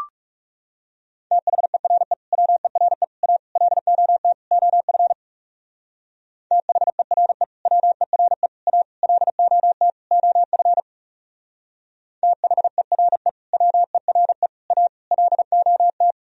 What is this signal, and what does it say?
Morse code, heard as a keyed tone, about 37 wpm, 700 Hz. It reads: THERE WERE A LOT OF THERE WERE A LOT OF THERE WERE A LOT
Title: Speed-Racing - Sets of 5 Words 30wpm